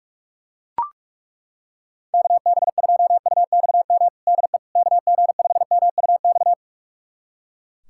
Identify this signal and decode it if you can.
Morse code, heard as a keyed tone, about 41 wpm, 700 Hz. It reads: KB2UXM DE KG5MUX
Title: USA Callsign DE Callsign - 6 Characters - 45wpm